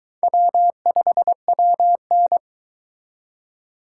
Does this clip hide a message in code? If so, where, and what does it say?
Yes: Morse code from 0.2 to 2.4 s, heard as a keyed tone, about 22 wpm, 700 Hz. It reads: W5WN